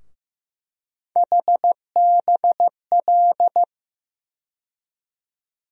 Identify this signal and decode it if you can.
Morse code, heard as a keyed tone, about 15 wpm, 700 Hz. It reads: HBL
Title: USA Call Sign Postfixes - 3 Letters 15wpm